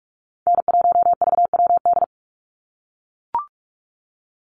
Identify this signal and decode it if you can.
Morse code, heard as a keyed tone, about 44 wpm, 700 Hz. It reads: N1VWD
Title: NA SOTA Chaser Callsigns 45wpm